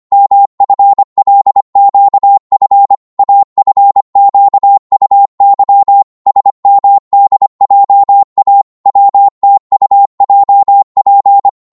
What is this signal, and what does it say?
Morse code, heard as a keyed tone, about 25 wpm, 800 Hz. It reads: MFLQF AFQUY SMDJA WTUJP
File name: Morse / ltr-25wpm